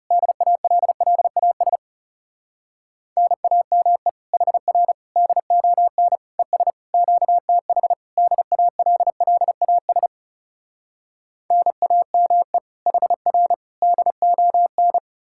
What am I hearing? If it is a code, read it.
Morse code, heard as a keyed tone, about 37 wpm, 700 Hz. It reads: DALLAS NAME HR DON ES QTH DALLAS NAME HR DON
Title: Speed-Racing Common QSO Phrases - 30wpm